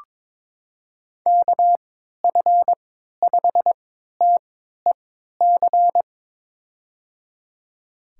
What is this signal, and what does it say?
Morse code, heard as a keyed tone, about 22 wpm, spaced as for 12 wpm, 700 Hz. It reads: KF5TEC
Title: USA Call Signs - 6 Characters - 3x Character Spacing - 22wpm